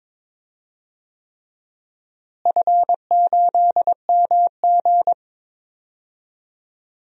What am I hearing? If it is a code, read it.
Morse code, heard as a keyed tone, about 22 wpm, 700 Hz. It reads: F8MG
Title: DXCC-100 Call Signs - 4 Characters - Repeated 3x - 22wpm